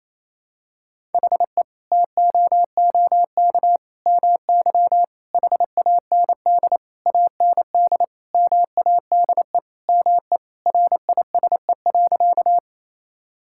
Morse code, heard as a keyed tone, about 28 wpm, 700 Hz. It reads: HE TOOK MY HAND AND MADE ME RISE.
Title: Sentences from Top 1000 Words - Review Part 6 - 28wpm